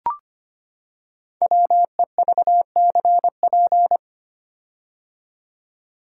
Morse code, heard as a keyed tone, about 25 wpm, 690 Hz. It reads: WEVCP
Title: Code Groups - 5 Letter:Numbers - 25wpm